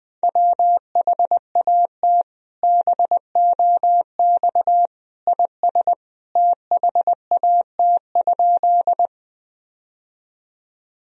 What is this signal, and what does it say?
Morse code, heard as a keyed tone, about 21 wpm, 700 Hz. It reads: WHAT BOX IS THAT?
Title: Sentences from Top 1000 Words - Review Part 2 - 20wpm